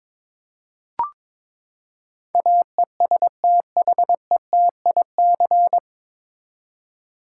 Morse code, heard as a keyed tone, about 22 wpm, 700 Hz. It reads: AESTHETIC
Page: Morse Code Ninja practice 22wpm